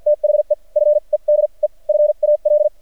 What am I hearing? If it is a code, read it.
Morse code, heard as a keyed tone, about 45 wpm, spaced as for 36 wpm, 600 Hz. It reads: THE FEDEFAL
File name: archived / CRSnet